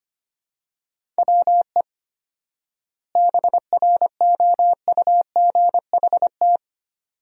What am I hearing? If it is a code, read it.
Morse code, heard as a keyed tone, about 25 wpm, 700 Hz. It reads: WE BROUGHT
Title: 4x Word-Spacing - Sentences from Top 500 Words - 25wpm